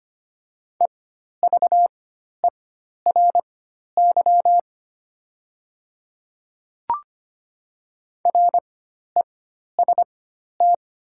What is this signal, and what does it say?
Morse code, heard as a keyed tone, about 25 wpm, spaced as for 12 wpm, 700 Hz. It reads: EVERY REST
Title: 4x Character Spacing - Top 500 Words - 25wpm